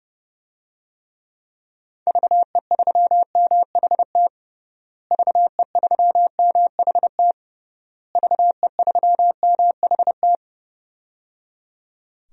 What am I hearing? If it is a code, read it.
Morse code, heard as a keyed tone, about 30 wpm, 700 Hz. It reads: VE3MHT VE3MHT VE3MHT
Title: DXCC-100 Call Signs - 6 Characters - Repeated 3x - 30wpm